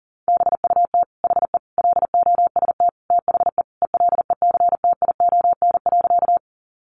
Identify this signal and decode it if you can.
Morse code, heard as a keyed tone, about 40 wpm, 700 Hz. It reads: BUT HE LOST THE ELECTION.
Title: Sentences from Top 1500 Words 40wpm